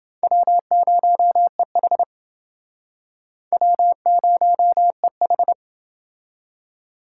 Morse code, heard as a keyed tone, about 28 wpm, 700 Hz. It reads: W0EH W0EH